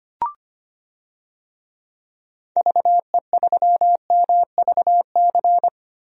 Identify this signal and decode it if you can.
Morse code, heard as a keyed tone, about 25 wpm, 700 Hz. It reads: VE3MVC
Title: DXCC-100 Call Signs - 6 Characters - Repeated 3x - 25wpm